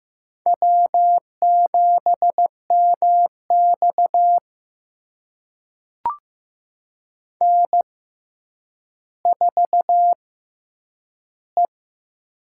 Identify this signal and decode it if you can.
Morse code, heard as a keyed tone, about 15 wpm, spaced as for 9 wpm, 700 Hz. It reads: W7MX N4E
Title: USA Call Signs - 4 Characters - 6x Character Spacing - 15wpm